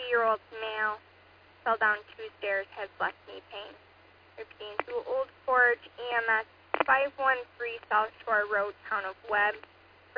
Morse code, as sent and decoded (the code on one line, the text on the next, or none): none